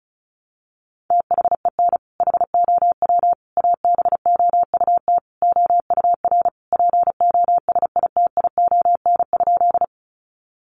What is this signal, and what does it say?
Morse code, heard as a keyed tone, about 34 wpm, 700 Hz. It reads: THEN HOW ABOUT OUR POSITION?
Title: Sentences from Top 900 Words 35wpm